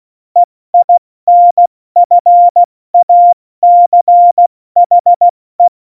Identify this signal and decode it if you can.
Morse code, heard as a keyed tone, about 16 wpm, 700 Hz. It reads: EINFACHE
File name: Morsekurs / Witze / Witz 15. September 2020, 70 F80